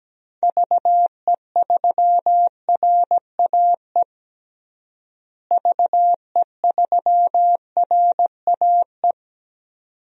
Morse code, heard as a keyed tone, about 17 wpm, 700 Hz. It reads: VE3RAE VE3RAE
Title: DXCC-100 Call Signs - 6 Characters - Repeated 2x - 17wpm